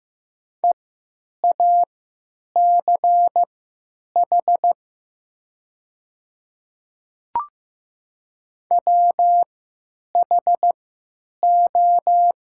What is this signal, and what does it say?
Morse code, heard as a keyed tone, about 15 wpm, spaced as for 8 wpm, 700 Hz. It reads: EACH WHO
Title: 3x Character Spacing - Top 100 Words - 15wpm